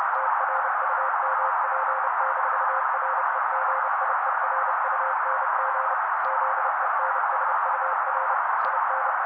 Morse code, heard as a keyed tone, about 30 wpm, 530 Hz. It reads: TRUMP-REGIERUNG ZENSUR IN
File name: daily / 29wpm noise